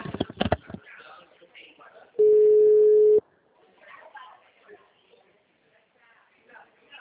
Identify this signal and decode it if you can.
Ringback tone (Europe, 425 Hz), with speech.
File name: out- -2002- - - .0